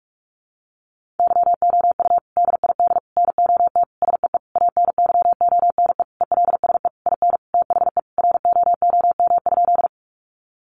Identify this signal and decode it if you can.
Morse code, heard as a keyed tone, about 44 wpm, 700 Hz. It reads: YOU DID NOT SEE ANYONE ELSE IN THE ROOM?
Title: Sentences from Top 900 Words 45wpm